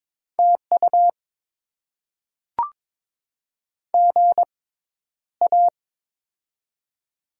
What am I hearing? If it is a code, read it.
Morse code, heard as a keyed tone, about 22 wpm, spaced as for 11 wpm, 700 Hz. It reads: TU GA
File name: Top 20 QSO Elements - 6x Character Spacing - 22wpm